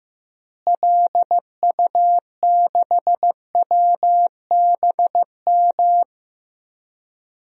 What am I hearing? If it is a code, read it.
Morse code, heard as a keyed tone, about 15 wpm, 700 Hz. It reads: LU6WBM
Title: DXCC-100 Call Signs - 6 Characters - Repeated 2x - 15wpm